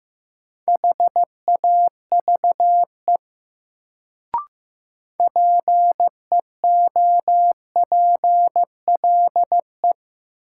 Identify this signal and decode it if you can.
Morse code, heard as a keyed tone, about 15 wpm, 700 Hz. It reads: HAVE PEOPLE
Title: Sets of 2 Words - New Word Drill - 15wpm